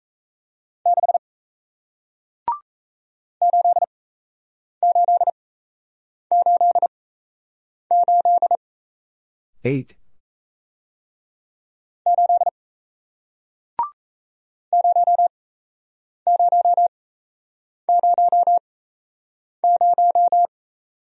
Morse code, heard as a keyed tone, about 38 wpm, spaced as for 8 wpm, 700 Hz. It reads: 6 8888 8 0000